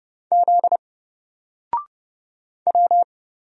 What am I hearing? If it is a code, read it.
Morse code, heard as a keyed tone, about 30 wpm, 700 Hz. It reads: Z W
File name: Single-Letter-Number-30wpm